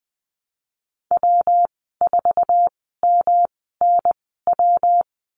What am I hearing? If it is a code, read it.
Morse code, heard as a keyed tone, about 19 wpm, spaced as for 14 wpm, 700 Hz. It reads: W4MNW
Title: 128 Lesson 21g - Number 6 - Call Signs 20wpm